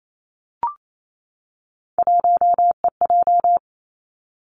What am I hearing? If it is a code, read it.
Morse code, heard as a keyed tone, about 28 wpm, 700 Hz. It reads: 1EJ
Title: Three Challenging Letter-Number Combinations 28wpm